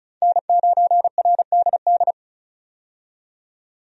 Morse code, heard as a keyed tone, about 36 wpm, 700 Hz. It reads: N9RDD
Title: USA Call Signs - 5 Characters - 35wpm